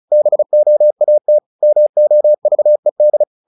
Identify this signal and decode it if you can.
Morse code, heard as a keyed tone, about 36 wpm, 600 Hz. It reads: BOAT MOVED
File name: the old man and the sea 2 35wpm